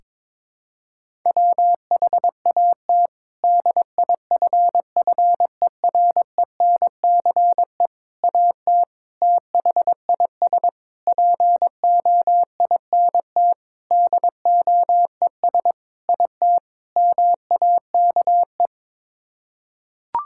Morse code, heard as a keyed tone, about 22 wpm, 700 Hz. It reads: WHAT DIFFERENCE AT THIS POINT DOES IT MAKE
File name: practice / 22wpm / Sets of 8 Words - 22wpm